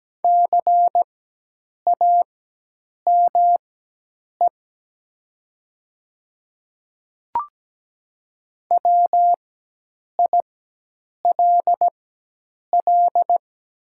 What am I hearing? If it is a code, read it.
Morse code, heard as a keyed tone, about 17 wpm, spaced as for 8 wpm, 700 Hz. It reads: CAME WILL
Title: 4x Character Spacing - Top 200 Words - 17wpm